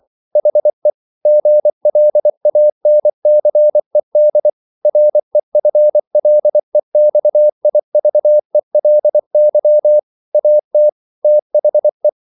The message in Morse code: HE GLANCED REFLEXIVELY AT THE

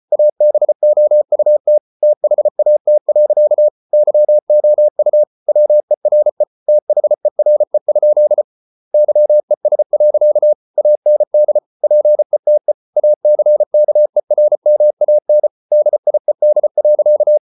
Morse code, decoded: ABOUT THAT. YOU WERE THERE? YES. AND PETE ACKERMAN DIED.

